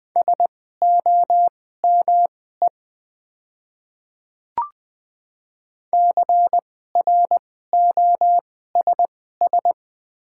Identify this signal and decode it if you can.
Morse code, heard as a keyed tone, about 20 wpm, spaced as for 14 wpm, 700 Hz. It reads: SOME CROSS
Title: 2x Character Spacing - Top 300 Words - 20wpm